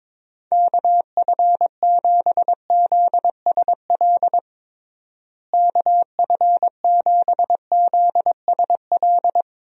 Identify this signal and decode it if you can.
Morse code, heard as a keyed tone, about 22 wpm, 700 Hz. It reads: KF7ZSL KF7ZSL
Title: USA Call Signs - 6 Characters - Repeated 2x - 22wpm